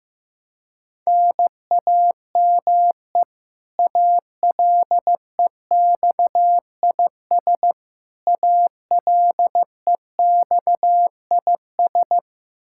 Morse code, heard as a keyed tone, about 15 wpm, 700 Hz. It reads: NAME ALEXIS ALEXIS
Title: Practice QSO - Exchange 1 - Beginning - 15wpm